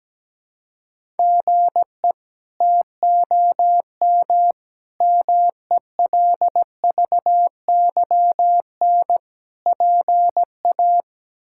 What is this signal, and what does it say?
Morse code, heard as a keyed tone, about 17 wpm, 700 Hz. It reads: GE TOM MELVYN PA
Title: K1USN SST QSO Practice - 17wpm